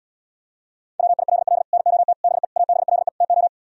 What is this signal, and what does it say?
Morse code, heard as a keyed tone, about 78 wpm, 700 Hz. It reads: WILL NOT BE ABLE TO